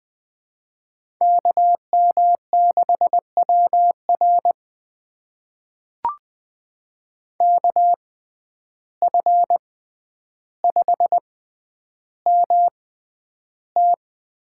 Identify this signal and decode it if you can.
Morse code, heard as a keyed tone, about 20 wpm, spaced as for 10 wpm, 700 Hz. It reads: KM6WR KF5MT
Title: USA Call Signs - 5 Characters - 6x Character Spacing - 20wpm